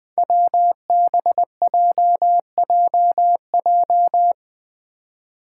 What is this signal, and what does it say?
Morse code, heard as a keyed tone, about 20 wpm, 700 Hz. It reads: WBJJJ
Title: Five Challenging Letter-Number Combinations 20wpm